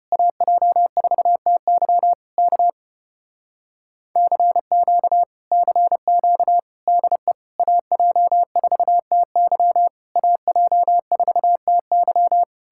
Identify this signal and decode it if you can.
Morse code, heard as a keyed tone, about 33 wpm, 700 Hz. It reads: AJ4TY K CQ CQ DE AJ4TY AJ4TY